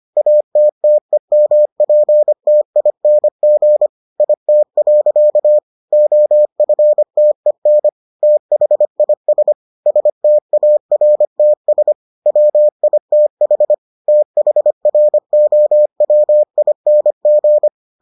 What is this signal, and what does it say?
Morse code, heard as a keyed tone, about 25 wpm, 600 Hz. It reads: ATTEMPTING IT. OFTEN THIS STARTS WITH THROWING